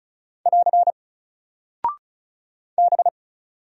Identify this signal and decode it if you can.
Morse code, heard as a keyed tone, about 35 wpm, 700 Hz. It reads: <AR> B
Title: Single Character (Letter:Number:Punctuation:Prosign) 35wpm